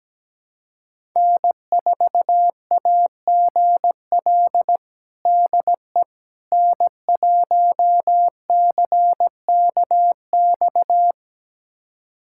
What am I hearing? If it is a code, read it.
Morse code, heard as a keyed tone, about 17 wpm, 700 Hz. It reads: N4AGL DE N1CKX
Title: USA Callsign DE Callsign - 5 Characters - 17wpm